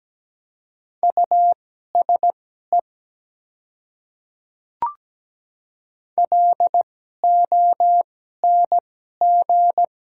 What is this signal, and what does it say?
Morse code, heard as a keyed tone, about 17 wpm, spaced as for 12 wpm, 700 Hz. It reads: USE LONG
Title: 2x Character Spacing - Top 100 Words - 17wpm